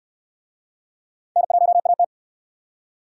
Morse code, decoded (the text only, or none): N1AT